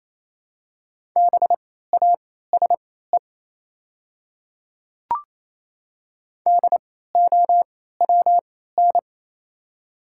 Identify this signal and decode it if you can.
Morse code, heard as a keyed tone, about 27 wpm, spaced as for 16 wpm, 700 Hz. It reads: BASE DOWN
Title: 3x Character Spacing - Top 300 Words - 28wpm